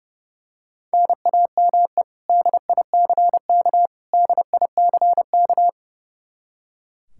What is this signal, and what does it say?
Morse code, heard as a keyed tone, about 30 wpm, 700 Hz. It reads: NAME DICK DICK
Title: Practice QSO - Exchange 1 - Beginning - 30wpm